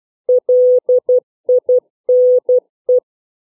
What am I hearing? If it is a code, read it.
Morse code, heard as a keyed tone, about 12 wpm, 500 Hz. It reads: LINE